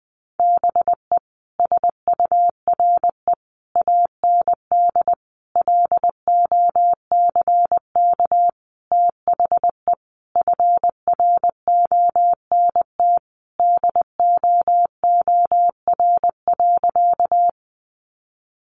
Morse code, heard as a keyed tone, about 20 wpm, 700 Hz. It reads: BE SURE AND LOCK THE FRONT DOOR.
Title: Sentences from Top 1300 Words 20wpm